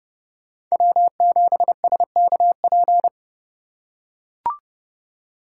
Morse code, heard as a keyed tone, about 28 wpm, 700 Hz. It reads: W7SKP